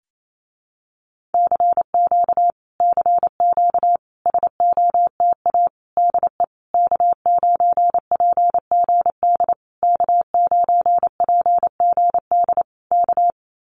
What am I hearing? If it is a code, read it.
Morse code, heard as a keyed tone, about 28 wpm, 700 Hz. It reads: CQ CQ SOTA DE K9PGD K9PGD K
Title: Common QSO Phrases Encore 28wpm